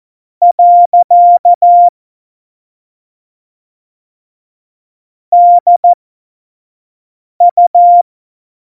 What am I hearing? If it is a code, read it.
Morse code, heard as a keyed tone, about 14 wpm, spaced as for 5 wpm, 700 Hz. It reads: . DU
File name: Morsekurs / Witze / Witz 20 March 2018, 25 F70